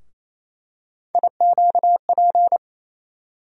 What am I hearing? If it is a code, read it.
Morse code, heard as a keyed tone, about 28 wpm, 700 Hz. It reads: IQP
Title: USA Call Sign Postfixes - 3 Letters 28wpm